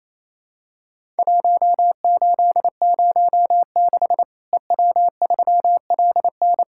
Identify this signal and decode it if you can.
Morse code, heard as a keyed tone, about 27 wpm, 700 Hz. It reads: 1806 EW3LN
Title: Worked All Europe DX Contest - QTC Exchange - 28wpm